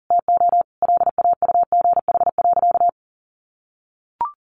Morse code, heard as a keyed tone, about 40 wpm, 700 Hz. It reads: TO LAUGH.
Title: Sentences from Top 600 Words 40wpm